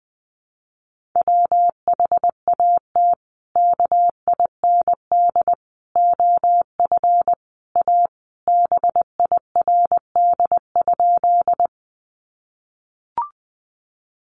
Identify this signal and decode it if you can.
Morse code, heard as a keyed tone, about 20 wpm, 700 Hz. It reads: WHAT KIND OF A BIRD?
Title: Sentences from Top 400 Words 20wpm